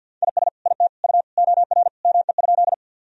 Morse code, heard as a keyed tone, about 46 wpm, 700 Hz. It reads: IS IT U OR ME?
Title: Sentences from Top 200 Words and Abbreviations 50wpm